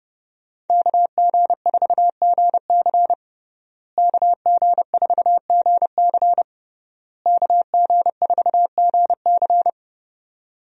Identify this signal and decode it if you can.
Morse code, heard as a keyed tone, about 30 wpm, 700 Hz. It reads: KG4GC KG4GC KG4GC
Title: USA Call Signs - 5 Characters - Repeated 3x - 30wpm